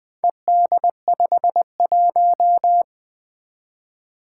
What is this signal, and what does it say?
Morse code, heard as a keyed tone, about 20 wpm, 700 Hz. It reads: ED51
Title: Four Challenging Letter-Number Combinations 20wpm